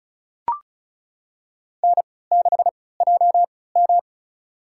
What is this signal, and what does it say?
Morse code, heard as a keyed tone, about 36 wpm, spaced as for 20 wpm, 700 Hz. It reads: N6JM
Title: USA Call Signs - 4 Characters - 3x Character Spacing - 35wpm